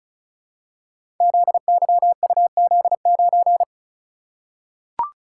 Morse code, heard as a keyed tone, about 36 wpm, 690 Hz. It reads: ZYUZ9
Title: Code Groups - 5 Letter:Numbers - 35wpm